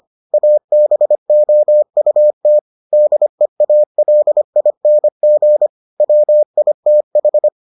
Morse code, heard as a keyed tone, about 25 wpm, 600 Hz. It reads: ABOUT DEALING WITH